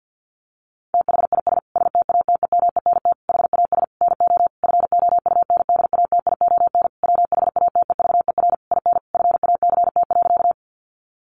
Morse code, heard as a keyed tone, about 50 wpm, 700 Hz. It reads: THIS STATEMENT HAS NO FOUNDATION WHATEVER IN FACT.
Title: Sentences from Top 1200 Words 50wpm